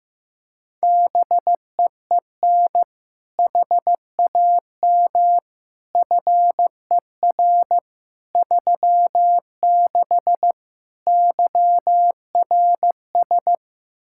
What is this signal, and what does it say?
Morse code, heard as a keyed tone, about 16 wpm, 700 Hz. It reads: BEEN HAM FER 36 YRS
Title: Practice QSO - Exchange 3 - More Info - 15wpm